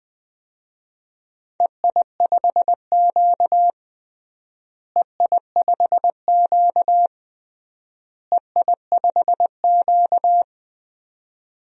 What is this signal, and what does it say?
Morse code, heard as a keyed tone, about 21 wpm, 700 Hz. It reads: EI5Q EI5Q EI5Q
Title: DXCC-100 Call Signs - 4 Characters - Repeated 3x - 20wpm